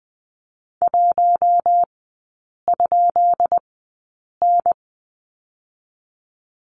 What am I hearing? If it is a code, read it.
Morse code, heard as a keyed tone, about 20 wpm, spaced as for 8 wpm, 700 Hz. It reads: 1?N
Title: 198 Lesson 32e - Letter k - Three Characters 20wpm